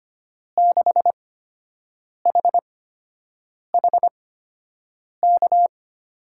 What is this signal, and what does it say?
Morse code, heard as a keyed tone, about 25 wpm, spaced as for 7 wpm, 700 Hz. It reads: 6HHK
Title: USA Call Signs - 6 Characters - 8x Character Spacing - 25wpm